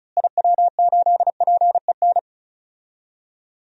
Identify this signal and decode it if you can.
Morse code, heard as a keyed tone, about 34 wpm, 700 Hz. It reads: IW8PEN
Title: DXCC-100 Call Signs - 6 Characters - Repeated 2x - 35wpm